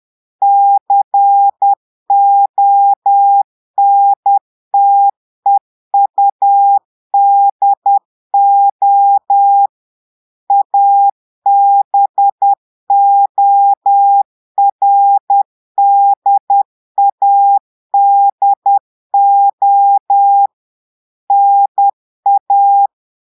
Morse code, heard as a keyed tone, about 10 wpm, 800 Hz. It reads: CONTEUDO ABORDADO NA
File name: portugais texte en clair 2 vitesse 10 mots minutes